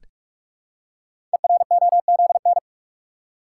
Morse code, heard as a keyed tone, about 44 wpm, 700 Hz. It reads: EROZN
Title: Code Groups - 5 Letters - 45wpm